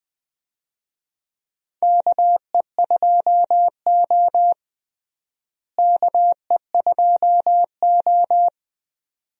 Morse code, heard as a keyed tone, about 20 wpm, 700 Hz. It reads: KE2O KE2O
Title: USA Call Signs - 4 Characters - Repeated 2x - 20wpm